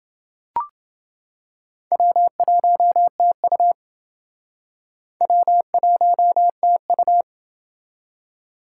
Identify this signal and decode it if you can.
Morse code, heard as a keyed tone, about 28 wpm, 700 Hz. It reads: W1TU W1TU